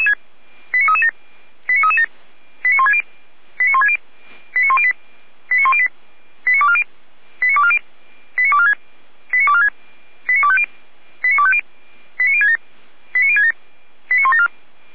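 Five-tone selective call (ZVEI), heard as quick runs of five tones, tones of about 80 ms. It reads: 89337 89337 89160 89160 89108 89108 89240 89240 89246 89246 89250 89250 89067 89067 89174
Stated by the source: ZVEI five-tone call received on SDR.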